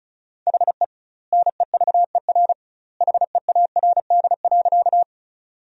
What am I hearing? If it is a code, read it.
Morse code, heard as a keyed tone, about 36 wpm, 700 Hz. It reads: HE NEVER HEARD.
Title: 2x Word-Spacing - Sentences from Top 400 Words - 35wpm